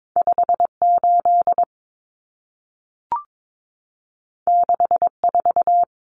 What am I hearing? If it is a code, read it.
Morse code, heard as a keyed tone, about 22 wpm, 700 Hz. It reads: 58 64
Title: Common Cut Numbers - 2-Digit 22wpm